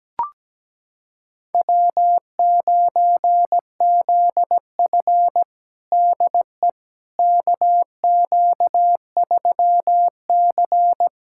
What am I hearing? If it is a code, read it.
Morse code, heard as a keyed tone, about 17 wpm, 700 Hz. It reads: W9ZF DE KQ3C
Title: USA Callsign DE Callsign - 4 Characters - 17wpm